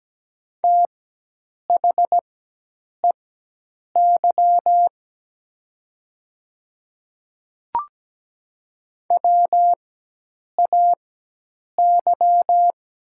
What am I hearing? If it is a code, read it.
Morse code, heard as a keyed tone, about 17 wpm, spaced as for 8 wpm, 700 Hz. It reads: THEY WAY
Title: 4x Character Spacing - Top 100 Words - 17wpm